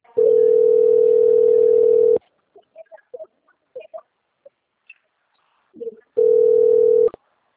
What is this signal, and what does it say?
Ringback tone (North America), 440 + 480 Hz.
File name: out- - - - - .29220